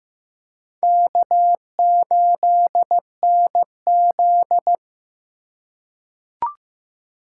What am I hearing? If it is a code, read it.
Morse code, heard as a keyed tone, about 15 wpm, 700 Hz. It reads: K8NZ
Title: K1USN SST QSO Practice - 15wpm